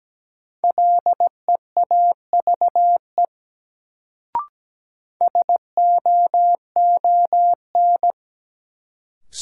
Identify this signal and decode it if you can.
Morse code, heard as a keyed tone, about 17 wpm, 700 Hz. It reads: LEAVE SOON
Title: Sentences from Top 400 Words - New Word Drill - 17wpm